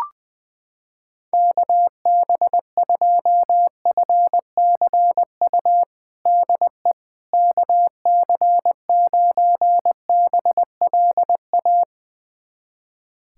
Morse code, heard as a keyed tone, about 20 wpm, 700 Hz. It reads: KB2FCU DE KC9BLA
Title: USA Callsign DE Callsign 20wpm